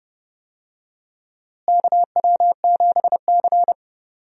Morse code, heard as a keyed tone, about 30 wpm, 700 Hz. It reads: KW7C
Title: USA Call Signs - 4 Characters - Repeated 2x - 30wpm